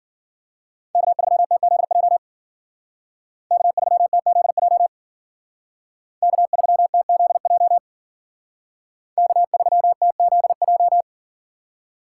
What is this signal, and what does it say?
Morse code, heard as a keyed tone, about 52 wpm, 700 Hz. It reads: K3TZJ K3TZJ K3TZJ K3TZJ